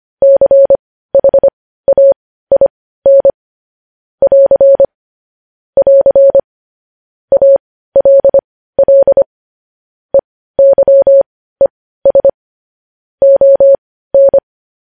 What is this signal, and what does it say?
Morse code, heard as a keyed tone, about 25 wpm, spaced as for 15 wpm, 550 Hz. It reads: CHAIN <AR> <AR> ALL EYES ON
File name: podcast / News headlines-1x-25x15